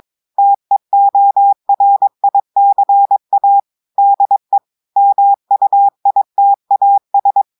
Morse code, heard as a keyed tone, about 22 wpm, 800 Hz. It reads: TEORICA DE MUITAS
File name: audios / portugais / portugais texte en clair 1 vitesse 22 mots minutes